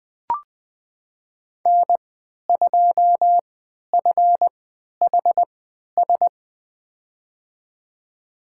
Morse code, heard as a keyed tone, about 20 wpm, spaced as for 11 wpm, 700 Hz. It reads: N2FHS
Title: USA Call Signs - 3x Character Spacing - 20wpm